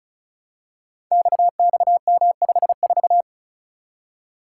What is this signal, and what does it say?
Morse code, heard as a keyed tone, about 36 wpm, 700 Hz. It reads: XXM54
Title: Code Groups - 5 Letter:Numbers - 35wpm